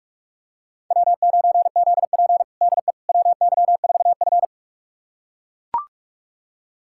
Morse code, heard as a keyed tone, about 44 wpm, 690 Hz. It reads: W9ZP DE WY4F
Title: USA Callsign DE Callsign - 4 Characters - 45wpm